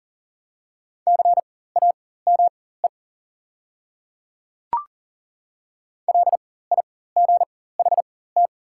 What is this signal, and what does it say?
Morse code, heard as a keyed tone, about 40 wpm, spaced as for 19 wpm, 690 Hz. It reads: CAME LIGHT
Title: 4x Character Spacing - Top 300 Words - 40wpm